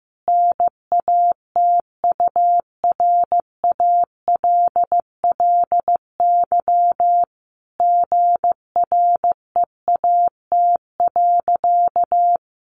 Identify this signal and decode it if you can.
Morse code, heard as a keyed tone, about 14 wpm, 700 Hz. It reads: NATURALLY GREAT.
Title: Sentences from Top 1300 Words 15wpm